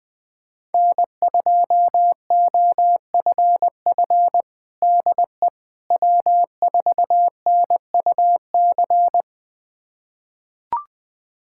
Morse code, heard as a keyed tone, about 20 wpm, 700 Hz. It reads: N2OFF DE W4NUC
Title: USA Callsign DE Callsign - 5 Characters - 20wpm